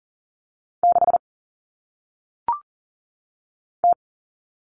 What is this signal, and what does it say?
Morse code, heard as a keyed tone, about 39 wpm, 700 Hz. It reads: 6 T
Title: Common Cut Numbers - 1-Digit 40wpm